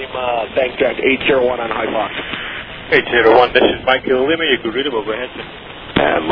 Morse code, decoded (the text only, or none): none